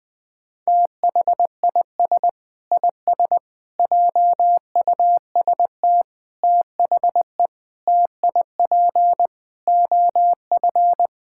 Morse code, heard as a keyed tone, about 20 wpm, 700 Hz. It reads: THIS IS JUST THE TIP OF
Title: Sets of 8 Words - 20wpm